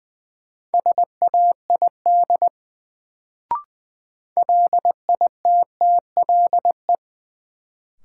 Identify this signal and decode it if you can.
Morse code, heard as a keyed tone, about 20 wpm, 700 Hz. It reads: SAID LITTLE
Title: Sets of 2 Words - New Word Drill - 20wpm